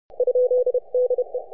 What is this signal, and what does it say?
Morse code, heard as a keyed tone, about 31 wpm, 500 Hz. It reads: ?DE
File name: n6tr